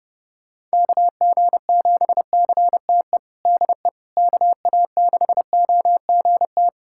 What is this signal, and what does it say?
Morse code, heard as a keyed tone, about 30 wpm, 700 Hz. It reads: KG7CTE DE KA6OGT
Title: USA Callsign DE Callsign - 6 Characters - 30wpm